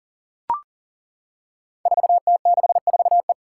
Morse code, heard as a keyed tone, about 40 wpm, 700 Hz. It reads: 4T64E
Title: Five Challenging Letter-Number Combinations 40wpm